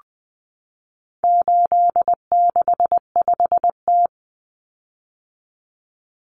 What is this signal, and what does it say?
Morse code, heard as a keyed tone, about 20 wpm, 700 Hz. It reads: 865T
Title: Common Cut Numbers - 4-Digit 20wpm